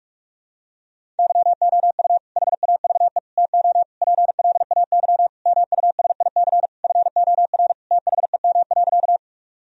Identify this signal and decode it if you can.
Morse code, heard as a keyed tone, about 42 wpm, 700 Hz. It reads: YOU HAVE TO PLAY MUSIC FOR THEM.
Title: Sentences from Top 1000 Words - Review Part 1 - 45wpm